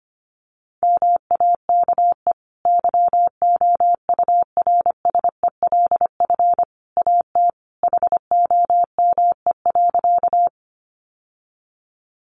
Morse code, heard as a keyed tone, about 25 wpm, 700 Hz. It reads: MAKE YOURSELF AT HOME.
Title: Sentences from Top 600 Words 25wpm